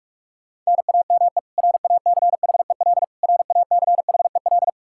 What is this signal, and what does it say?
Morse code, heard as a keyed tone, about 41 wpm, 700 Hz. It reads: NAME RACHEL RACHEL
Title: Practice QSO - Exchange 1 - Beginning - 45wpm